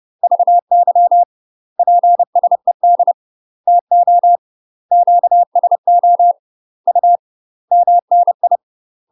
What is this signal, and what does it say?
Morse code, heard as a keyed tone, about 30 wpm, 700 Hz. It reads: VY PSED TO QSO U MNI
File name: UK MASTER QSOs TRAINING / 30WPM / UK MASTER QSOs 0004